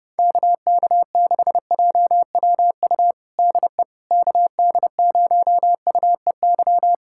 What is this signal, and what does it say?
Morse code, heard as a keyed tone, about 31 wpm, 700 Hz. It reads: KK6JWU DE KD0UEY
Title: USA Callsign DE Callsign 30wpm